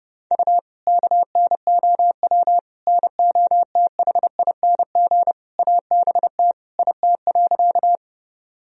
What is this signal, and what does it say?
Morse code, heard as a keyed tone, about 30 wpm, 700 Hz. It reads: U KNOW NOTHING ABT IT.
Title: Sentences from Top 500 Words and Abbreviations 30wpm